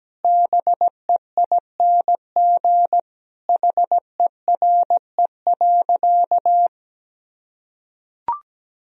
Morse code, heard as a keyed tone, about 17 wpm, 700 Hz. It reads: BEING HERE.
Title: Sentences from Top 1000 Words - Review Part 3 - 17wpm